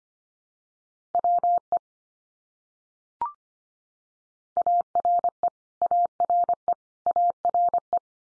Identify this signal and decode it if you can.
Morse code, heard as a keyed tone, about 25 wpm, 700 Hz. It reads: WE ARE ARE ARE
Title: Top 100 Words 25wpm x3